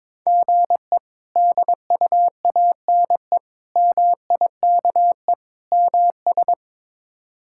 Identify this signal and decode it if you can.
Morse code, heard as a keyed tone, about 22 wpm, 700 Hz. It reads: GE DUANE MIKE MS
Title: K1USN SST QSO Practice - 22wpm